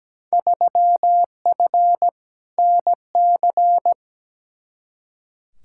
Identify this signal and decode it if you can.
Morse code, heard as a keyed tone, about 17 wpm, 700 Hz. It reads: 3F NC
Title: Field Day Exchanges Encore - 17wpm